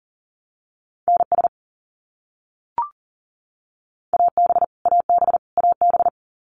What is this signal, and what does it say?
Morse code, heard as a keyed tone, about 40 wpm, 700 Hz. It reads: NS AB AB AB
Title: Canadian Provinces and Territories - Repeated 3x - 40wpm